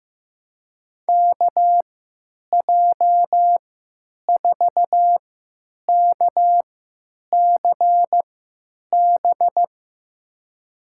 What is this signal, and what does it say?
Morse code, heard as a keyed tone, about 15 wpm, spaced as for 9 wpm, 700 Hz. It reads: KJ4KCB
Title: USA Call Signs - 6 Characters - 3x Character Spacing - 15wpm